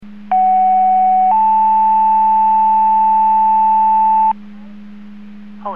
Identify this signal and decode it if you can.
Two-tone page, heard as a one-second tone, then a three-second tone, about 750 then 880 Hz. It is heard from 0.3 to 4.3 s.
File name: Portage 154.385 Paging